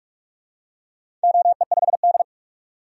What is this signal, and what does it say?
Morse code, heard as a keyed tone, about 45 wpm, 700 Hz. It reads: OE5D
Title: DXCC-100 Call Signs - 4 Characters - 45wpm